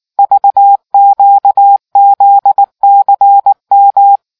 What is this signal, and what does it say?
Morse code, heard as a keyed tone, about 18 wpm, 800 Hz. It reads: VQZCM